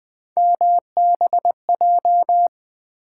Morse code, heard as a keyed tone, about 20 wpm, 700 Hz. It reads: MBJ
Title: Three Letter-Number Combinations - 20wpm